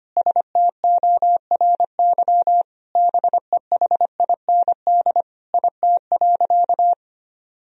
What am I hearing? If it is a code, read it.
Morse code, heard as a keyed tone, about 25 wpm, 700 Hz. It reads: STORY BEHIND IT.
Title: Sentences from Top 500 Words and Abbreviations 25wpm